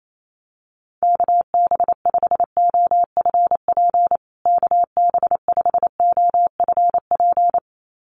Morse code, heard as a keyed tone, about 28 wpm, 700 Hz. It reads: KB5OFP KB5OFP
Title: USA Call Signs Repeated 2x 28wpm